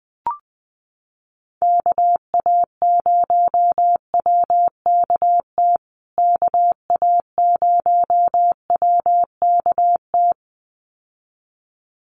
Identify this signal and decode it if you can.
Morse code, heard as a keyed tone, about 20 wpm, 700 Hz. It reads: KA0WKT KA0WKT
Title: USA Call Signs Repeated 2x 20wpm